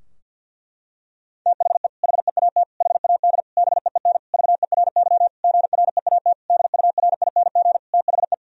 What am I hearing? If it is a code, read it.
Morse code, heard as a keyed tone, about 50 wpm, 700 Hz. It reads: THE HEAT HAD BEEN VERY GREAT DURING THE
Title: Sentences from Top 1000 Words - Review Part 3 - 50wpm